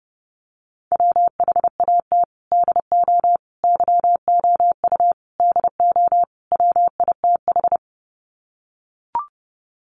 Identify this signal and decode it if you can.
Morse code, heard as a keyed tone, about 30 wpm, 700 Hz. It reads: WHAT DO YOU DO WITH
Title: Sets of 5 Words Encore2 30wpm